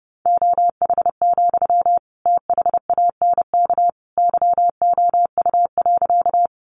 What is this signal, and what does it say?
Morse code, heard as a keyed tone, about 30 wpm, 700 Hz. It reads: OH, THANK YOU.
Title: Sentences from Top 600 Words 30wpm